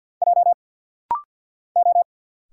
Morse code, heard as a keyed tone, about 50 wpm, 690 Hz. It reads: J O
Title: Single Letter-Number Alt - Rapid-Fire 50wpm